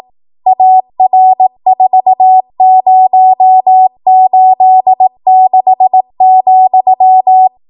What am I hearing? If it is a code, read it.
Morse code, heard as a keyed tone, about 18 wpm, 750 Hz. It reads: AR4086,
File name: Morse / 18WPM